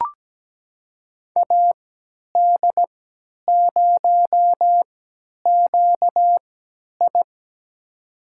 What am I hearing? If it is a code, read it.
Morse code, heard as a keyed tone, about 17 wpm, spaced as for 10 wpm, 700 Hz. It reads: AD0QI
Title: USA Call Signs - 5 Characters - 3x Character Spacing - 17wpm